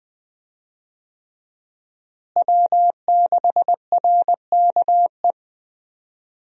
Morse code, heard as a keyed tone, about 21 wpm, 700 Hz. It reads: W6RKE
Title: USA Call Signs - 5 Characters - Repeated 3x - 20wpm